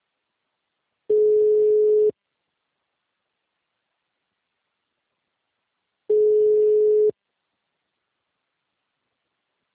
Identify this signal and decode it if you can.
Ringback tone (Europe, 425 Hz).